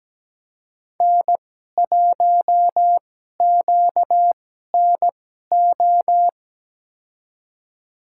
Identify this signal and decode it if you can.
Morse code, heard as a keyed tone, about 17 wpm, spaced as for 12 wpm, 700 Hz. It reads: N1QNO
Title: USA Call Signs - 2x Character Spacing - 17wpm